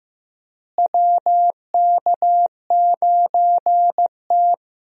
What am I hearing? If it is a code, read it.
Morse code, heard as a keyed tone, about 15 wpm, 700 Hz. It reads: WK9T
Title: USA Call Signs - 4 Characters - Repeated 2x - 15wpm